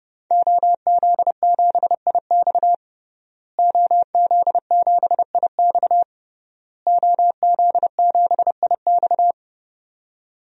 Morse code, heard as a keyed tone, about 30 wpm, 700 Hz. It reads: OZ7IX OZ7IX OZ7IX
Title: DXCC-100 Call Signs - 5 Characters - Repeated 3x - 30wpm